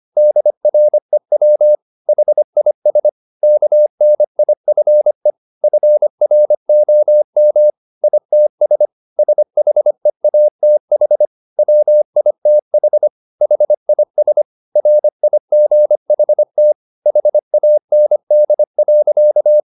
Morse code, heard as a keyed tone, about 25 wpm, 600 Hz. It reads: DREW HIS KNIFE FROM ITS SHEATH WITH HIS RIGHT HAND.